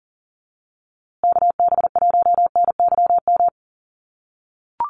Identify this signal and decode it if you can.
Morse code, heard as a keyed tone, about 40 wpm, 700 Hz. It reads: KB1NYM
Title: USA Call Signs Repeated 2x 40wpm